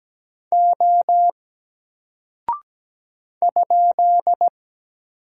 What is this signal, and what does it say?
Morse code, heard as a keyed tone, about 17 wpm, 700 Hz. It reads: O ?
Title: Single Character (Letter:Number:Punctuation:Prosign) 17wpm